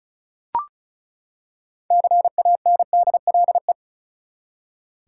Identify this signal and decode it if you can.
Morse code, heard as a keyed tone, about 36 wpm, 700 Hz. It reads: CANDLE
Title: Six Letter Words - 35wpm